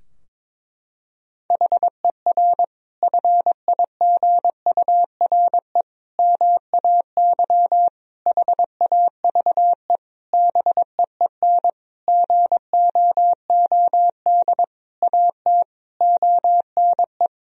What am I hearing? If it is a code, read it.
Morse code, heard as a keyed tone, about 22 wpm, 700 Hz. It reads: HER FIGURE MAY HAVE BEEN GOOD AT ONE
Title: Sentences from Top 1000 Words - Review Part 2 - 22wpm